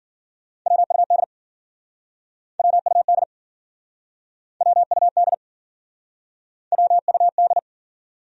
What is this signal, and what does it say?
Morse code, heard as a keyed tone, about 52 wpm, 700 Hz. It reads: WUD WUD WUD WUD